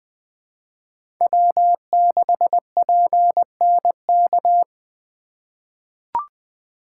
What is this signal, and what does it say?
Morse code, heard as a keyed tone, about 20 wpm, 700 Hz. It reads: W6PNK
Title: USA Call Signs - 5 Characters - 4x Character Spacing - 20wpm